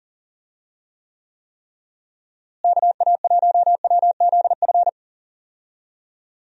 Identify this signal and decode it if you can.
Morse code, heard as a keyed tone, about 40 wpm, 700 Hz. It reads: KA1WZF
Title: USA Call Signs - 6 Characters - Repeated 3x - 40wpm